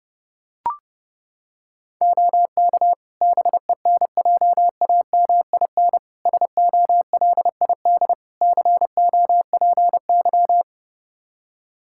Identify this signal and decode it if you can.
Morse code, heard as a keyed tone, about 31 wpm, 700 Hz. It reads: OK BENJAMIN SOLID COPY
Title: Practice QSO - Exchange 3 - More Info - 30wpm